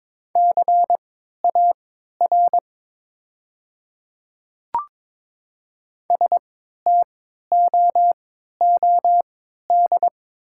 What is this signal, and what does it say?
Morse code, heard as a keyed tone, about 23 wpm, spaced as for 13 wpm, 690 Hz. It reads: CAR STOOD
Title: 3x Character Spacing - Top 500 Words - 22wpm